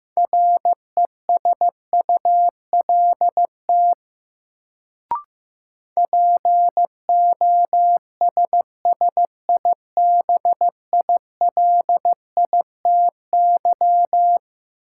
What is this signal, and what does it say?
Morse code, heard as a keyed tone, about 15 wpm, 700 Hz. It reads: RESULT POSSIBILITY
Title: Sets of 3 Words - New Word Drill - 15wpm